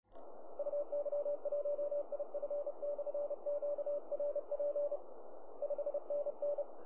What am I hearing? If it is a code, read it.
Morse code, heard as a keyed tone, about 33 wpm, 580 Hz, with radio noise. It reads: UY1IF/QRP 5NN
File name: attachments / QSK